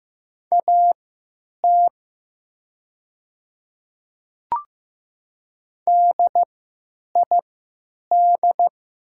Morse code, heard as a keyed tone, about 15 wpm, spaced as for 9 wpm, 700 Hz. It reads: AT DID
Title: 3x Character Spacing - Top 100 Words - 15wpm